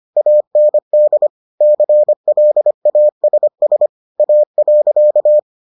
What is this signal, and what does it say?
Morse code, heard as a keyed tone, about 25 wpm, 600 Hz. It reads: AND CLASS A.